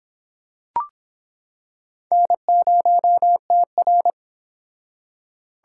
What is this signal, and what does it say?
Morse code, heard as a keyed tone, about 26 wpm, 700 Hz. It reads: N0TR